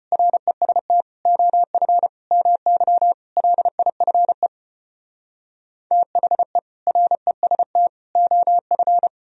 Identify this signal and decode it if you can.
Morse code, heard as a keyed tone, about 33 wpm, 700 Hz. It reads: REST OF MY LIFE THE REST OF